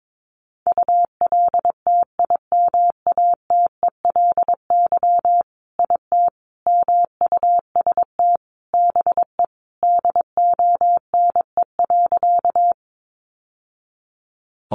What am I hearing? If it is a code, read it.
Morse code, heard as a keyed tone, about 22 wpm, 700 Hz. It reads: ULTIMATELY IT MUST BE DONE.
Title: Sentences from Top 1200 Words 22wpm